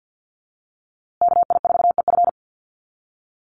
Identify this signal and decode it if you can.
Morse code, heard as a keyed tone, about 50 wpm, 700 Hz. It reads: KI4EF
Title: USA Call Signs Repeated 2x 50wpm